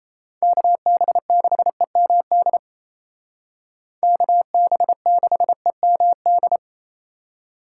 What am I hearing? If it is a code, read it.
Morse code, heard as a keyed tone, about 35 wpm, 700 Hz. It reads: KB6EMD KB6EMD
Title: Speed Racing - USA 2x3 Call Signs 28wpm